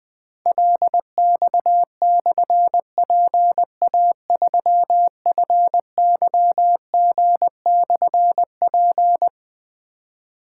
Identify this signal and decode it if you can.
Morse code, heard as a keyed tone, about 20 wpm, 700 Hz. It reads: LX/PA3FYG/P
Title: Difficult Call Signs 20wpm